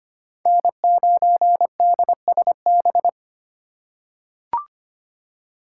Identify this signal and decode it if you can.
Morse code, heard as a keyed tone, about 25 wpm, 700 Hz. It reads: N9DSB
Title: USA Call Signs - 5 Characters - 2x Character Spacing - 25wpm